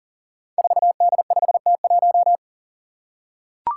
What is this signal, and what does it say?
Morse code, heard as a keyed tone, about 40 wpm, 700 Hz. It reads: 4D5T1
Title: Five Challenging Letter-Number Combinations 40wpm